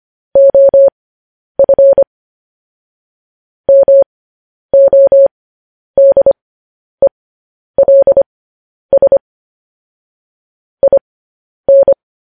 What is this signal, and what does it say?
Morse code, heard as a keyed tone, about 25 wpm, spaced as for 10 wpm, 550 Hz. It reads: OF MODELS IN
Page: news-headlines-in-morse-code-at-10-wpm